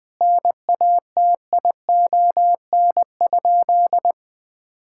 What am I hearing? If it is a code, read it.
Morse code, heard as a keyed tone, about 21 wpm, 700 Hz. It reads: NATION?
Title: Sentences from Top 1000 Words - Review Part 6 - 20wpm